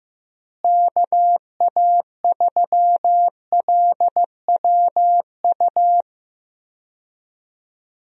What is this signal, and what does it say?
Morse code, heard as a keyed tone, about 15 wpm, 700 Hz. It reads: KA3LWU
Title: USA Call Signs - 6 Characters - Repeated 2x - 15wpm